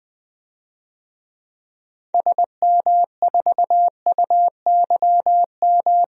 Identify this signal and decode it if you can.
Morse code, heard as a keyed tone, about 20 wpm, 700 Hz. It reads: SM4UYM
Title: DXCC-100 Call Signs - 6 Characters - Repeated 2x - 20wpm